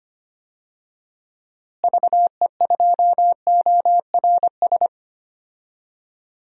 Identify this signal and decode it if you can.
Morse code, heard as a keyed tone, about 25 wpm, 700 Hz. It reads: VE2ORS
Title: DXCC-100 Call Signs - 6 Characters - Repeated 3x - 25wpm